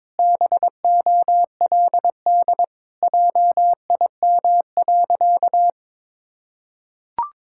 Morse code, heard as a keyed tone, about 22 wpm, 700 Hz. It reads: BOLD JIM.
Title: Pangram Challenge v2 - 22wpm